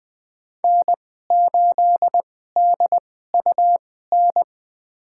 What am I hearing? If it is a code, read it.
Morse code, heard as a keyed tone, about 21 wpm, spaced as for 15 wpm, 700 Hz. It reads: N8DUN
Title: USA Call Signs - 5 Characters - 2x Character Spacing - 20wpm